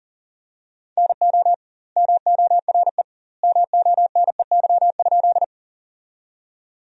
Morse code, heard as a keyed tone, about 40 wpm, 700 Hz. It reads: NO MORE MONEY?
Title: 2x Word-Spacing - Sentences from Top 400 Words - 40wpm